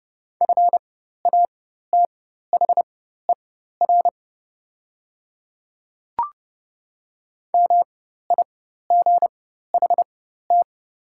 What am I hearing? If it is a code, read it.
Morse code, heard as a keyed tone, about 30 wpm, spaced as for 14 wpm, 700 Hz. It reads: FATHER MIGHT
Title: 4x Character Spacing - Top 300 Words - 30wpm